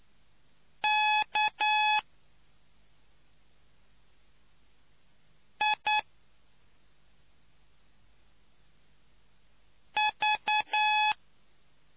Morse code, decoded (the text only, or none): K I V